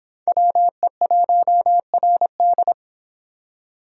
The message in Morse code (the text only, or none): WE1RD